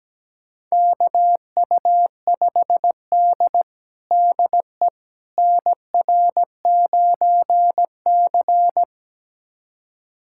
Morse code, heard as a keyed tone, about 17 wpm, 700 Hz. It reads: KU5D DE NR9C
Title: USA Callsign DE Callsign - 4 Characters - 17wpm